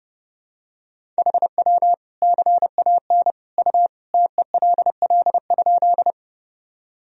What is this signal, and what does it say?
Morse code, heard as a keyed tone, about 30 wpm, 700 Hz. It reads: HW CAN U TELL?
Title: Sentences from Top 200 Words and Abbreviations 30wpm